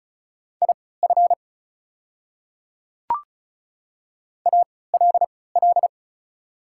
Morse code, heard as a keyed tone, about 36 wpm, spaced as for 20 wpm, 700 Hz. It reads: IF ALL
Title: 3x Character Spacing - Top 100 Words - 35wpm